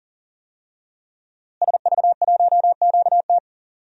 Morse code, heard as a keyed tone, about 40 wpm, 700 Hz. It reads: SV1QT
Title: DXCC-100 Call Signs - 5 Characters - Repeated 3x - 40wpm